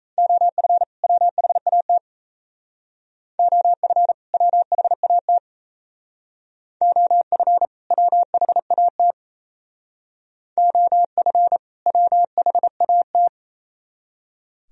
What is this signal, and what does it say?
Morse code, heard as a keyed tone, about 33 wpm, 700 Hz. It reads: OF WHAT OF WHAT OF WHAT OF WHAT
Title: Speed-Racing - Sets of 2 Words 28wpm